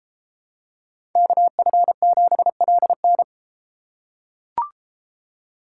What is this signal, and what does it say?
Morse code, heard as a keyed tone, about 32 wpm, 700 Hz. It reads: KF7LN